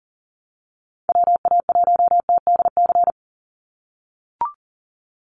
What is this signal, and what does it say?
Morse code, heard as a keyed tone, about 40 wpm, 700 Hz. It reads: WA1TDC
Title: USA Call Signs 40wpm